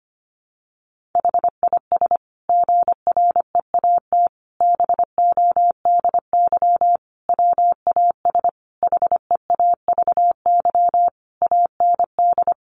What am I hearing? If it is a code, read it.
Morse code, heard as a keyed tone, about 25 wpm, 700 Hz. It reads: HIS GREAT BODY WAS HEAVY AND
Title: Sentences from Top 1400 Words 25wpm